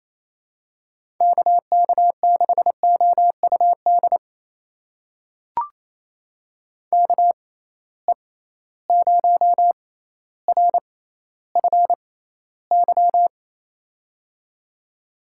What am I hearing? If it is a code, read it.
Morse code, heard as a keyed tone, about 27 wpm, spaced as for 14 wpm, 700 Hz. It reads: KK6OUD KE0RFY
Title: USA Call Signs - 6 Characters - 6x Character Spacing - 28wpm